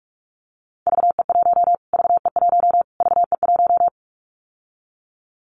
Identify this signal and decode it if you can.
Morse code, heard as a keyed tone, about 44 wpm, 700 Hz. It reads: VE1 VE1 VE1
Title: Callsign Prefixes for Top 25 DXCC - Repeated 3x - 45wpm